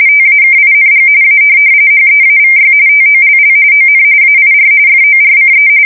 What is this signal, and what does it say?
RTTY, about 45.45 baud. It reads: TAKING A HEAVY TOLL ON AIR FORCE
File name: fsk170 / USS-New-Jersey